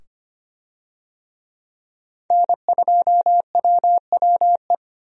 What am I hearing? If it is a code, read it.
Morse code, heard as a keyed tone, about 25 wpm, 700 Hz. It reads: N2WWE
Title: USA Call Signs - 5 Characters - 25wpm